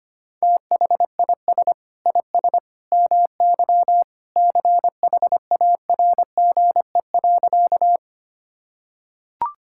Morse code, heard as a keyed tone, about 25 wpm, 700 Hz. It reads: THIS IS MY CHARGE.
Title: Sentences from Top 1000 Words - Review Part 6 - 25wpm